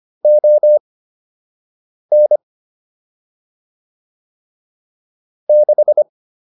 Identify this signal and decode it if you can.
Morse code, heard as a keyed tone, about 25 wpm, spaced as for 6 wpm, 600 Hz. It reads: ON 6